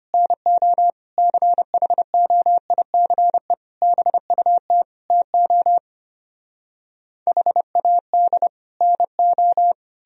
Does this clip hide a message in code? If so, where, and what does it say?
Yes: Morse code from 0.1 to 9.7 s, heard as a keyed tone, about 33 wpm, 700 Hz. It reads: NO CHOICE BUT TO HAD NO